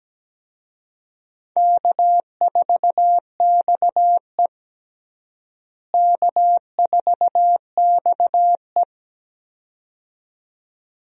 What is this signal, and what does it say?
Morse code, heard as a keyed tone, about 17 wpm, 700 Hz. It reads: K4XE K4XE
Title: USA Call Signs - 4 Characters - Repeated 2x - 17wpm